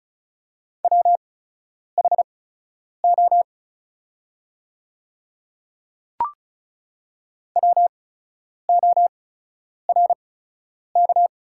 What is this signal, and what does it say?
Morse code, heard as a keyed tone, about 36 wpm, spaced as for 10 wpm, 700 Hz. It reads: WHO WORK
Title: 8x Character Spacing - Top 300 Words - 35wpm